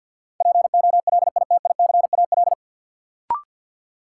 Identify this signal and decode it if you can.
Morse code, heard as a keyed tone, about 52 wpm, 700 Hz. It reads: POLITICAL
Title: Sentences from Top 800 Words - New Word Drill - 50wpm